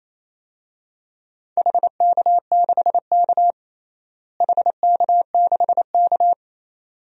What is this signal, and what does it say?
Morse code, heard as a keyed tone, about 28 wpm, 700 Hz. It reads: HK6K HK6K
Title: DXCC-100 Call Signs - 4 Characters - Repeated 2x - 28wpm